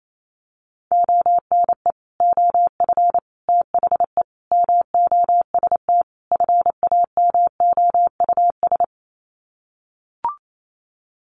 Morse code, heard as a keyed tone, about 28 wpm, 700 Hz. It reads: ONE OF THE MOST FAMOUS
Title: Sets of 5 Words Encore2 28wpm